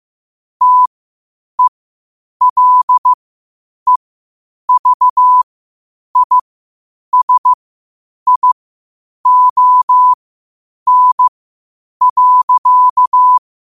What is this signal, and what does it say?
Morse code, heard as a keyed tone, about 15 wpm, spaced as for 8 wpm, 1000 Hz. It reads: TELEVISION.
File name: qotdfiles / 8wpm